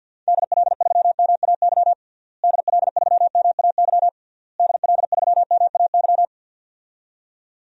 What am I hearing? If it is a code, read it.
Morse code, heard as a keyed tone, about 50 wpm, 700 Hz. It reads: DL3MAY DL3MAY DL3MAY
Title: DXCC-100 Call Signs - 6 Characters - Repeated 3x - 50wpm